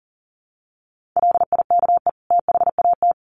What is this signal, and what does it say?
Morse code, heard as a keyed tone, about 37 wpm, 700 Hz. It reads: LIKE THAT
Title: Sets of 2 Words 40wpm